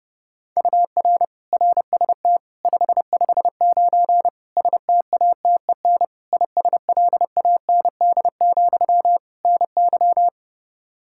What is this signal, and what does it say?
Morse code, heard as a keyed tone, about 30 wpm, 700 Hz. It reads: UR RST 559 STATEN ISLAND, NY
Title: Practice QSO - Exchange 1 - Beginning - 30wpm